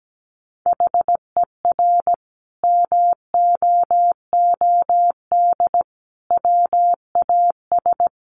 Morse code, heard as a keyed tone, about 17 wpm, 700 Hz. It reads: HER MOOD WAS
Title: Sentences from Top 1100 Words 17wpm